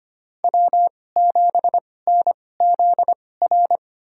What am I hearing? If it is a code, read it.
Morse code, heard as a keyed tone, about 25 wpm, spaced as for 18 wpm, 700 Hz. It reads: W7NZR
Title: USA Call Signs - 5 Characters - 2x Character Spacing - 25wpm